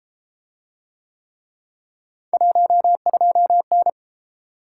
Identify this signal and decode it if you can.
Morse code, heard as a keyed tone, about 33 wpm, 700 Hz. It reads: 12N